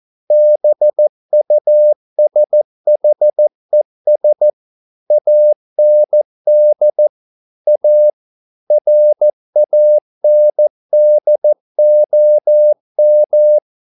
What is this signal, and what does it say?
Morse code, heard as a keyed tone, about 14 wpm, 600 Hz. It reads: BUSHES AND A RANDOM